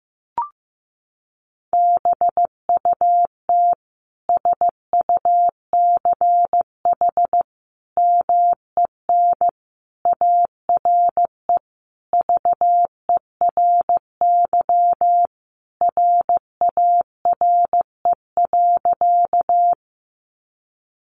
Morse code, heard as a keyed tone, about 15 wpm, 700 Hz. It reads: BUT SUCH MEN ARE VERY RARE.
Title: Sentences from Top 1100 Words 15wpm